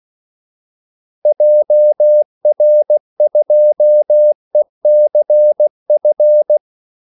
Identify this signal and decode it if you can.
Morse code, heard as a keyed tone, about 16 wpm, 600 Hz. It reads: JR2ECF
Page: Kaldesignaler 16 wpm